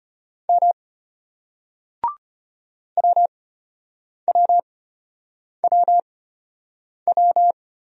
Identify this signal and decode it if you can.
Morse code, heard as a keyed tone, about 33 wpm, spaced as for 8 wpm, 700 Hz. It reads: M WWWW